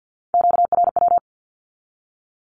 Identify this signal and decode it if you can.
Morse code, heard as a keyed tone, about 50 wpm, 700 Hz. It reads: QRW
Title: Q-Codes with Spoken Meaning - 50wpm